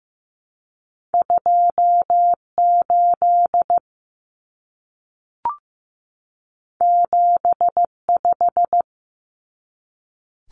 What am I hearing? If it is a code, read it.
Morse code, heard as a keyed tone, about 15 wpm, 700 Hz. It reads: 28 75
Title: Common Cut Numbers - 2-Digit 15wpm